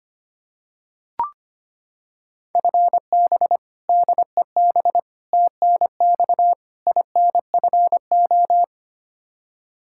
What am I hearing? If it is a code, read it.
Morse code, heard as a keyed tone, about 25 wpm, 700 Hz. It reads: FB DEB TNX INFO
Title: Practice QSO - Exchange 3 - More Info - 25wpm